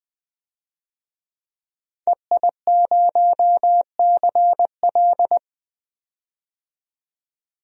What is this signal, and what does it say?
Morse code, heard as a keyed tone, about 20 wpm, 700 Hz. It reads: EI0CL
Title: DXCC-100 Call Signs - 5 Characters - 20wpm